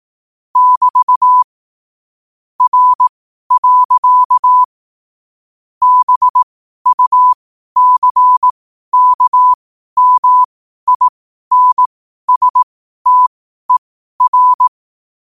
Morse code, heard as a keyed tone, about 18 wpm, spaced as for 12 wpm, 1000 Hz. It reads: <BT> R. BUCKMINSTER
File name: qotdfiles / 12wpm